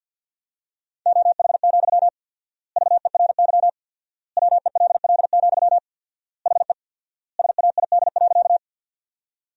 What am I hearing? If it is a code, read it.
Morse code, heard as a keyed tone, about 50 wpm, 700 Hz. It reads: OH, VERY WELL, HE SAID.
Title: 4x Word-Spacing - Sentences from Top 500 Words - 50wpm